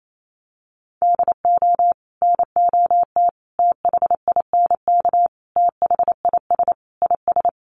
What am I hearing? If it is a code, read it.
Morse code, heard as a keyed tone, about 28 wpm, 700 Hz. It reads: DO NOT THINK THIS IS
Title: Sets of 5 Words Encore2 28wpm